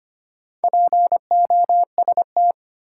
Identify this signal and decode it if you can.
Morse code, heard as a keyed tone, about 25 wpm, 700 Hz. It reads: POST
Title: Sentences from Top 1000 Words - New Word Drill - 25wpm